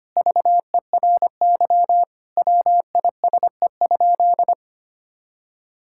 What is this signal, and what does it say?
Morse code, heard as a keyed tone, about 25 wpm, 700 Hz. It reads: VERY WISE?
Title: Sentences from Top 1600 Words 25wpm